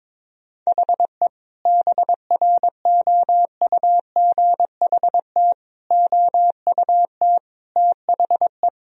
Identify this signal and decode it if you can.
Morse code, heard as a keyed tone, about 22 wpm, 700 Hz. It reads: HE BROUGHT OUT THE
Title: Sentences from Top 1000 Words - Review Part 3 - 22wpm